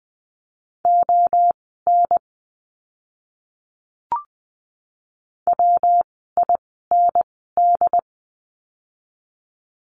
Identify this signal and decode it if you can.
Morse code, heard as a keyed tone, about 20 wpm, spaced as for 15 wpm, 700 Hz. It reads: ON WIND
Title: 115 Lesson 19f - Letter w - Words 20wpm